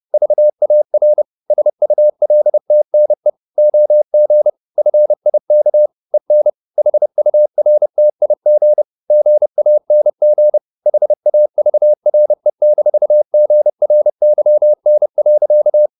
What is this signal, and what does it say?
Morse code, heard as a keyed tone, about 30 wpm, 600 Hz. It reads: VAR SULTNE OG FIK EN HURTIG GANG HAVRE-GRYN.
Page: Ferie 30 wpm